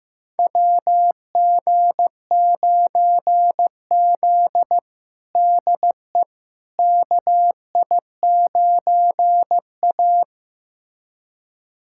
Morse code, heard as a keyed tone, about 15 wpm, 700 Hz. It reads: WG9Z DE KI9A
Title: USA Callsign DE Callsign - 4 Characters - 15wpm